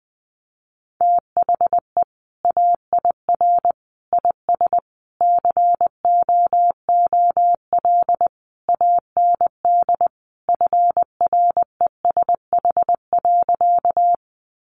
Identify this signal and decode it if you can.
Morse code, heard as a keyed tone, about 20 wpm, 700 Hz. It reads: THE AIR IS COOL AND FRESH.
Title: Sentences from Top 1200 Words 20wpm